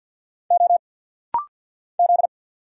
Morse code, heard as a keyed tone, about 50 wpm, 700 Hz. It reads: O Z
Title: Single Letters - Rapid-Fire 50wpm